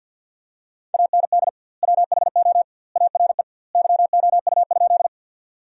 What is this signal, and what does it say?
Morse code, heard as a keyed tone, about 49 wpm, 700 Hz. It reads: AND WHO ARE YOU?
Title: 2x Word-Spacing - Sentences from Top 100 Words - 50wpm